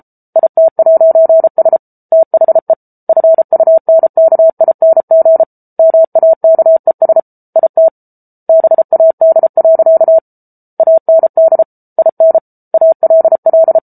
Morse code, heard as a keyed tone, about 34 wpm, 650 Hz. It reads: IT'S THE FUNKING MAKES IT BAD. AND IN ALL